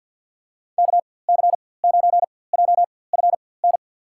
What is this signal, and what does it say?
Morse code, heard as a keyed tone, about 47 wpm, spaced as for 23 wpm, 700 Hz. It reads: KC9JFN
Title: USA Call Signs - 6 Characters - 4x Character Spacing - 50wpm